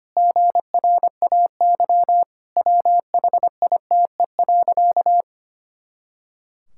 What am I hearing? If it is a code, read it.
Morse code, heard as a keyed tone, about 25 wpm, 700 Hz. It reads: GRAY WHITE.
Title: Pangram Challenge v2 - 25wpm